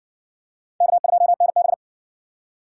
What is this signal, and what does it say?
Morse code, heard as a keyed tone, about 53 wpm, 700 Hz. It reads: K2NB